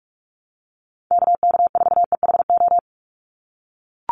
Morse code, heard as a keyed tone, about 44 wpm, 700 Hz. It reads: KK4EHO
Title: USA Call Signs 45wpm